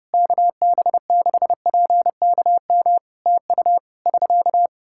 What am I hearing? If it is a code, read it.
Morse code, heard as a keyed tone, about 30 wpm, 700 Hz. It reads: KB6PKM TU <SK>
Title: Practice QSO - Exchange 4 - Ending - 30wpm